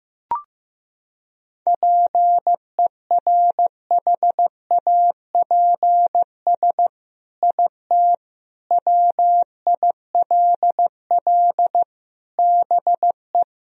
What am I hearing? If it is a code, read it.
Morse code, heard as a keyed tone, about 15 wpm, 700 Hz. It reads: PERHAPS IT WILL BE
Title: Sentences from Top 1000 Words - Review Part 3 - 15wpm